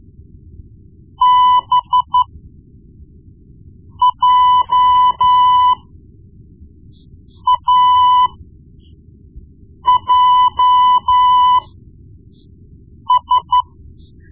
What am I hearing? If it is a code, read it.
Morse code, heard as a keyed tone, about 13 wpm, spaced as for 4 wpm, 1000 Hz. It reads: BJAJS
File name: Morse Lessons / Morse Lessons / Tape-1B